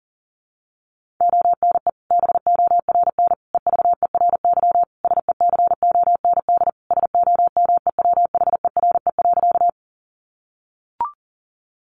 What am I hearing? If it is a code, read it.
Morse code, heard as a keyed tone, about 40 wpm, 700 Hz. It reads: ONE BORN EVERY SECOND SOMEWHERE.
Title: Sentences from Top 1400 Words 40wpm